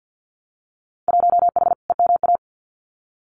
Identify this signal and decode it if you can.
Morse code, heard as a keyed tone, about 50 wpm, 700 Hz. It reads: 1H EMA
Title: Winter Field Day Exchanges Encore 50wpm